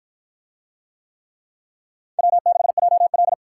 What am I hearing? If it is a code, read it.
Morse code, heard as a keyed tone, about 44 wpm, 700 Hz. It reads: W6JL
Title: Speed Racing - USA 1x2 Call Signs 35wpm